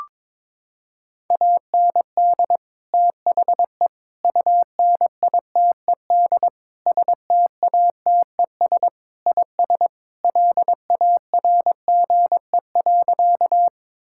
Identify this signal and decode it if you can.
Morse code, heard as a keyed tone, about 21 wpm, 700 Hz. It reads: AND THE UNITED STATES IS LARGE.
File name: practice / 22wpm / Sentences with 4-Word Phrases - 22wpm